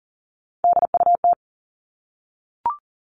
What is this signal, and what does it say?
Morse code, heard as a keyed tone, about 40 wpm, 700 Hz. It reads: DUT
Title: All Cut Numbers - 3-Digit 40wpm